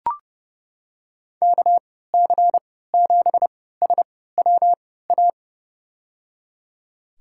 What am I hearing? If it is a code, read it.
Morse code, heard as a keyed tone, about 30 wpm, spaced as for 17 wpm, 700 Hz. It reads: KC7SWA
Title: USA Call Signs - 6 Characters - 3x Character Spacing - 30wpm